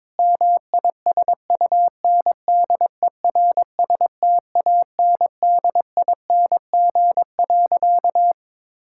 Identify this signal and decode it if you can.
Morse code, heard as a keyed tone, about 22 wpm, 700 Hz. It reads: MISUNDERSTANDING.
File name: practice / 22wpm / Sentences with 4-Word Phrases - 22wpm